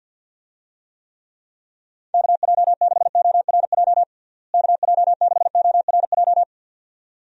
Morse code, heard as a keyed tone, about 51 wpm, 700 Hz. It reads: KJ6ORJ KJ6ORJ
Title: USA Call Signs - 6 Characters - Repeated 2x - 50wpm